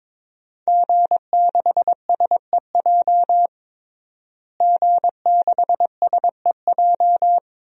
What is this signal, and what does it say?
Morse code, heard as a keyed tone, about 22 wpm, 700 Hz. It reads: G6SEJ G6SEJ
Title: DXCC-100 Call Signs - 5 Characters - Repeated 2x - 22wpm